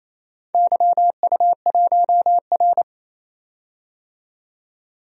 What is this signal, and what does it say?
Morse code, heard as a keyed tone, about 27 wpm, 700 Hz. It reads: YU1R
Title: DXCC-100 Call Signs - 4 Characters - 28wpm